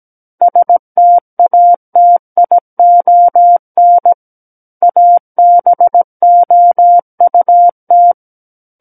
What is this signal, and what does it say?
Morse code, heard as a keyed tone, about 17 wpm, 700 Hz. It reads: STATION ABOUT
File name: Book One / 08-Chapter 8 - 17 WPM